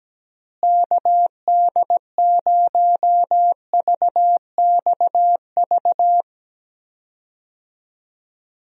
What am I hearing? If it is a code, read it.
Morse code, heard as a keyed tone, about 17 wpm, 700 Hz. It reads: KD0VXV
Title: USA Call Signs - 6 Characters - Repeated 3x - 17wpm